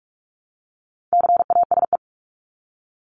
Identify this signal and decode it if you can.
Morse code, heard as a keyed tone, about 44 wpm, 700 Hz. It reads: CASE
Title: Top 501-750 Words 45wpm